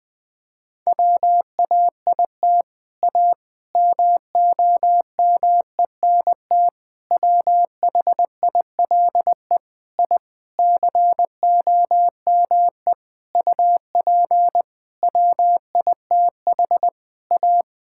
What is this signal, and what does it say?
Morse code, heard as a keyed tone, about 20 wpm, 700 Hz. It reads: WAIT A MOMENT WHILE I COME UP WITH A
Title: Sentences from Top 1000 Words - Review Part 5 - 20wpm